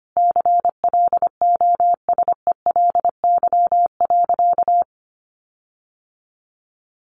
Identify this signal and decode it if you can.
Morse code, heard as a keyed tone, about 25 wpm, 700 Hz. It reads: CLOSELY.
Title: Sentences from Top 1400 Words 25wpm